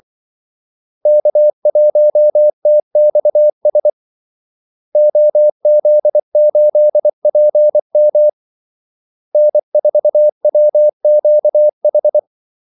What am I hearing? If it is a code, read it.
Morse code, heard as a keyed tone, about 24 wpm, 600 Hz. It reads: K1TXS OZ8PM N4WQH
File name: Callsigns-24wpm